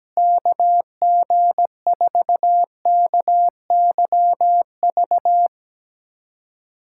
Morse code, heard as a keyed tone, about 17 wpm, 700 Hz. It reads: KG4KYV
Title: USA Callsign DE Callsign - 6 Characters - 17wpm